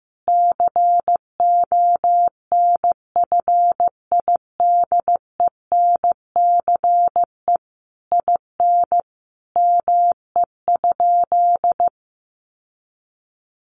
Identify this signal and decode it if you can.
Morse code, heard as a keyed tone, about 15 wpm, 700 Hz. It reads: CONFIDENCE IN ME?
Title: Sentences from Top 1500 Words 15wpm